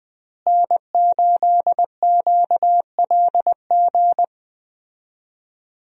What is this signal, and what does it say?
Morse code, heard as a keyed tone, about 20 wpm, 700 Hz. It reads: N8QLG
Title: USA Call Signs - 5 Characters - Repeated 2x - 20wpm